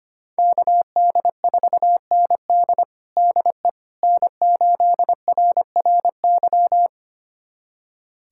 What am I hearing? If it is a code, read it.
Morse code, heard as a keyed tone, about 25 wpm, 700 Hz. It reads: KD4ND DE N8RRY
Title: USA Callsign DE Callsign - 5 Characters - 25wpm